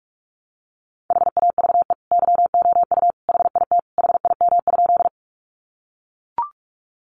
Morse code, heard as a keyed tone, about 44 wpm, 700 Hz. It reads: HAVE YOU HIT HIM?
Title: Sentences from Top 1200 Words 45wpm